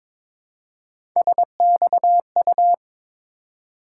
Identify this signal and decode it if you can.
Morse code, heard as a keyed tone, about 22 wpm, 700 Hz. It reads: SXU
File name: practice / 22wpm / Three Letter-Number Combinations - 22wpm